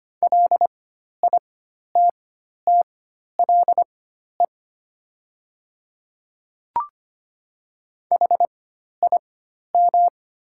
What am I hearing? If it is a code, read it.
Morse code, heard as a keyed tone, about 25 wpm, spaced as for 12 wpm, 700 Hz. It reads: LITTLE HIM
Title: 4x Character Spacing - Top 300 Words - 25wpm